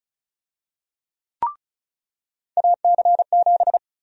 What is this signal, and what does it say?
Morse code, heard as a keyed tone, about 36 wpm, 700 Hz. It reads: AC7
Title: USA Call Sign Prefixes - 35wpm